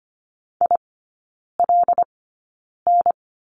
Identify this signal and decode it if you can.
Morse code, heard as a keyed tone, about 25 wpm, spaced as for 9 wpm, 700 Hz. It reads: ILN
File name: Lesson 14e - Number 2 - Three Characters 25wpm